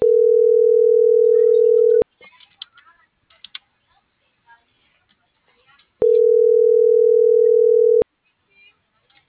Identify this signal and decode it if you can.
Ringback tone (North America), 440 + 480 Hz.